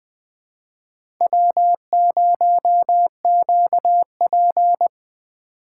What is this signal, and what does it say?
Morse code, heard as a keyed tone, about 20 wpm, 700 Hz. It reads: W0QP
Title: USA Call Signs - 4 Characters - 3x Character Spacing - 20wpm